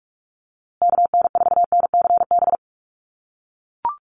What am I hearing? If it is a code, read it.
Morse code, heard as a keyed tone, about 44 wpm, 690 Hz. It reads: KN4NCB
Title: USA Call Signs Repeated 3x 45wpm